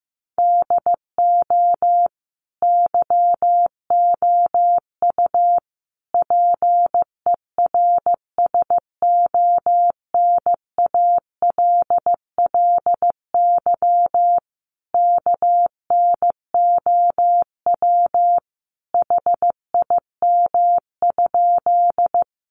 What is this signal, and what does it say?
Morse code, heard as a keyed tone, about 15 wpm, 700 Hz. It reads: DO YOU PERSONALLY KNOW HIM?
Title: Sentences from Top 1400 Words 15wpm